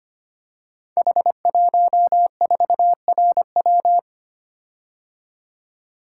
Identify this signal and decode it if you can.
Morse code, heard as a keyed tone, about 25 wpm, 700 Hz. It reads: H14RW
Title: Code Groups - 5 Letter:Numbers - 25wpm